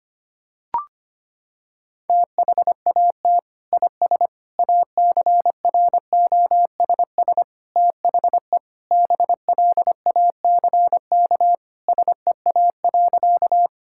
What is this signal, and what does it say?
Morse code, heard as a keyed tone, about 25 wpm, 700 Hz. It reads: THAT IS ACROSS THE BLACK SEA.
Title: Sentences from Top 1000 Words - Review Part 5 - 25wpm